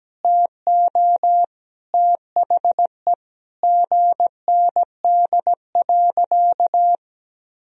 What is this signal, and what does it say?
Morse code, heard as a keyed tone, about 17 wpm, 700 Hz. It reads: TO THE GND.
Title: Sentences from Top 500 Words and Abbreviations 17wpm